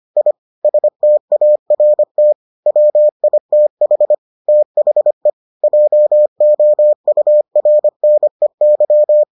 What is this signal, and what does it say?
Morse code, heard as a keyed tone, about 25 wpm, 600 Hz. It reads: I START WITH THE JOURNEY